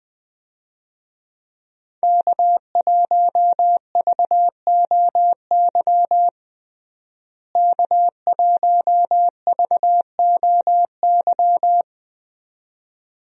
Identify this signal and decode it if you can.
Morse code, heard as a keyed tone, about 20 wpm, 700 Hz. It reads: K1VOY K1VOY
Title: USA Call Signs - 5 Characters - Repeated 2x - 20wpm